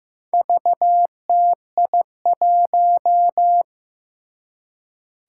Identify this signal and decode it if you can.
Morse code, heard as a keyed tone, about 15 wpm, 700 Hz. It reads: VTI1
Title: Four Challenging Letter-Number Combinations 15wpm